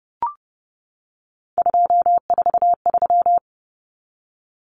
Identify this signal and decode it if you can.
Morse code, heard as a keyed tone, about 30 wpm, 700 Hz. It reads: 243
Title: Numbers - 3-Digit 30wpm